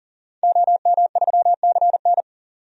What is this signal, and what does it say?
Morse code, heard as a keyed tone, about 40 wpm, 700 Hz. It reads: OM3CN
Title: DXCC-100 Call Signs - 5 Characters - 40wpm